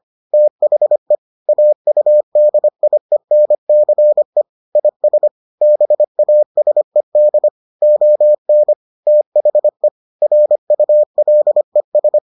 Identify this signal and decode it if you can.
Morse code, heard as a keyed tone, about 25 wpm, 600 Hz. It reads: THE AUDIENCE IS BASED ON THE RULES